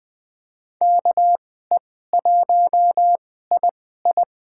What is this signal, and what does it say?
Morse code, heard as a keyed tone, about 20 wpm, spaced as for 14 wpm, 700 Hz. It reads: KE1II
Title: USA Call Signs - 5 Characters - 2x Character Spacing - 20wpm